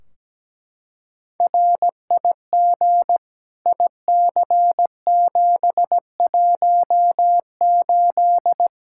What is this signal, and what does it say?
Morse code, heard as a keyed tone, about 17 wpm, 700 Hz. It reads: RIG IC718
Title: K1USN SST QSO Practice 17wpm